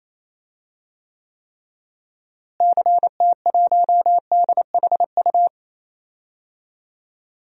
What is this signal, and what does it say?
Morse code, heard as a keyed tone, about 28 wpm, 700 Hz. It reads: CT1DHU
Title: DXCC-100 Call Signs - 6 Characters - Repeated 3x - 28wpm